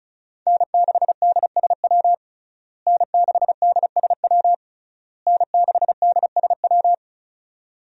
Morse code, heard as a keyed tone, about 36 wpm, 700 Hz. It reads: N6DSW N6DSW N6DSW
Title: USA Call Signs - 5 Characters - Repeated 3x - 35wpm